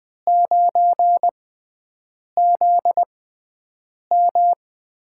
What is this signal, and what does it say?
Morse code, heard as a keyed tone, about 20 wpm, spaced as for 7 wpm, 700 Hz. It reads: 9ZM
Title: USA Call Signs - 5 Characters - 6x Character Spacing - 20wpm